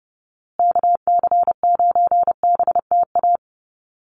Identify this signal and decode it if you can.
Morse code, heard as a keyed tone, about 30 wpm, 700 Hz. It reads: KC9BTA
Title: USA Call Signs Repeated 2x 30wpm